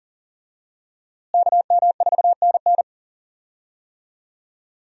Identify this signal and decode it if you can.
Morse code, heard as a keyed tone, about 40 wpm, 700 Hz. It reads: KM4NN
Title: USA Call Signs - 5 Characters - Repeated 2x - 40wpm